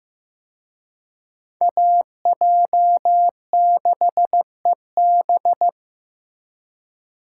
Morse code, heard as a keyed tone, about 18 wpm, 700 Hz. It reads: AJ6EB
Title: Speed Racing - USA 2x2 Call Signs 15wpm